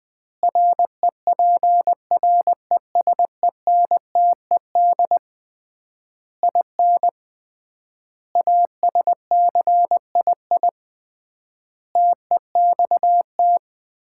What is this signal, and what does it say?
Morse code, heard as a keyed tone, about 20 wpm, 700 Hz. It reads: REPRESENTED IN ASCII TEXT
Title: 3x Word-Spacing - Sets of 5 Words - 20wpm